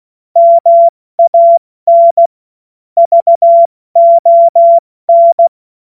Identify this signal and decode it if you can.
Morse code, heard as a keyed tone, about 16 wpm, 680 Hz. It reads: MAN VON
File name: Morsekurs / Witze / Witz 29. Juni 2021, 70 F80